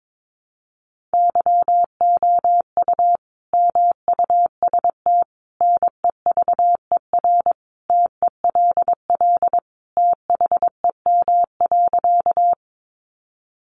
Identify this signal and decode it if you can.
Morse code, heard as a keyed tone, about 22 wpm, 700 Hz. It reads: YOU MUST NEVER TELL THEM.
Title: Sentences from Top 900 Words 22wpm